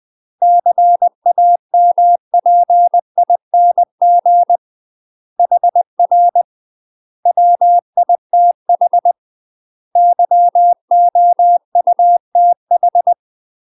Morse code, heard as a keyed tone, about 20 wpm, 700 Hz. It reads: CAMPING HR WITH YOUTH
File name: UK MASTER QSOs TRAINING / 20WPM / UK MASTER QSOs 0020